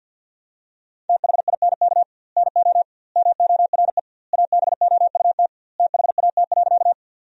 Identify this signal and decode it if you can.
Morse code, heard as a keyed tone, about 51 wpm, 700 Hz. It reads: THINK NO MORE ABOUT THAT.
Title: 2x Word-Spacing - Sentences from Top 200 Words - 50wpm